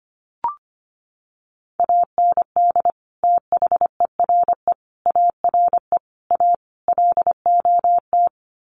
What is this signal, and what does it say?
Morse code, heard as a keyed tone, about 25 wpm, 700 Hz. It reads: AND THERE ARE A LOT
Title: Sets of 5 Words Encore2 25wpm